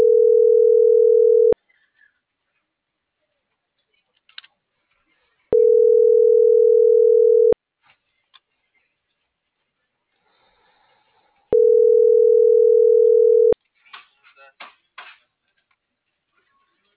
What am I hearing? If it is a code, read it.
Ringback tone (North America), 440 + 480 Hz.